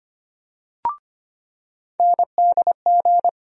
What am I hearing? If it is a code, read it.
Morse code, heard as a keyed tone, about 25 wpm, 700 Hz. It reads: NDG
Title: USA Call Sign Postfixes - 3 Letters 25wpm